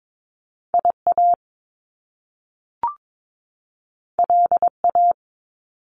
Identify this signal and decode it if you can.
Morse code, heard as a keyed tone, about 22 wpm, 700 Hz. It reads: IA LA
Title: US States 22wpm